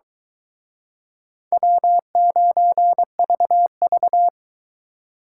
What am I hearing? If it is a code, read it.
Morse code, heard as a keyed tone, about 23 wpm, 700 Hz. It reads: W9VV